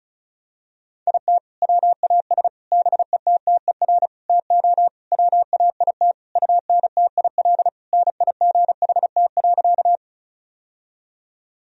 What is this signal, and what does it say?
Morse code, heard as a keyed tone, about 36 wpm, 700 Hz. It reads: IT WAS BETTER TO WAIT UNTIL NIGHT.
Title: Sentences from Top 1000 Words - Review Part 1 - 35wpm